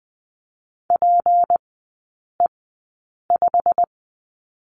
Morse code, heard as a keyed tone, about 20 wpm, spaced as for 8 wpm, 700 Hz. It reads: PE5
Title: 218 Lesson 35e - Number 0 - Three Characters 20wpm